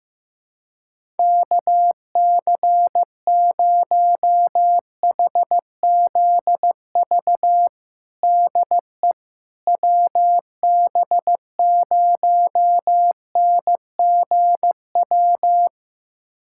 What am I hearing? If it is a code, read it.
Morse code, heard as a keyed tone, about 16 wpm, 700 Hz. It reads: KC0HZV DE WB0NGW
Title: USA Callsign DE Callsign - 6 Characters - 15wpm